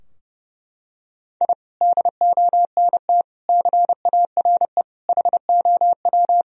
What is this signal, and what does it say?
Morse code, heard as a keyed tone, about 30 wpm, 700 Hz. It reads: I DONT CARE HOW
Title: Sentences from Top 1000 Words - Review Part 3 - 30wpm